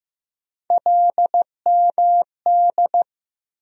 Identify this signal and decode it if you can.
Morse code, heard as a keyed tone, about 15 wpm, 700 Hz. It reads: LMD
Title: USA Call Sign Postfixes - 3 Letters 15wpm